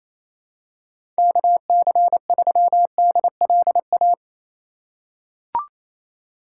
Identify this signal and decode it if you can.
Morse code, heard as a keyed tone, about 28 wpm, 700 Hz. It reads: KC3DLA
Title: USA Call Signs - 6 Characters - 2x Character Spacing - 28wpm